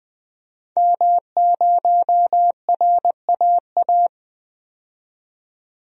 Morse code, heard as a keyed tone, about 20 wpm, 700 Hz. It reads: M0RAA
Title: DXCC-100 Call Signs - 5 Characters - 20wpm